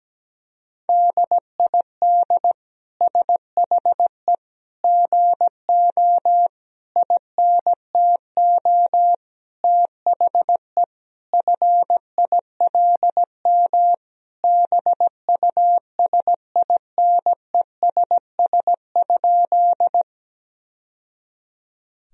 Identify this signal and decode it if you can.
Morse code, heard as a keyed tone, about 17 wpm, 700 Hz. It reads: DID SHE GO INTO THE FILM BUSINESS?
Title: Sentences from Top 1000 Words - Review Part 3 - 17wpm